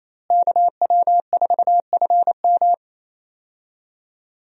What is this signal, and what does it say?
Morse code, heard as a keyed tone, about 28 wpm, 700 Hz. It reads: KW4FM
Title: USA Call Signs - 5 Characters - Repeated 3x - 28wpm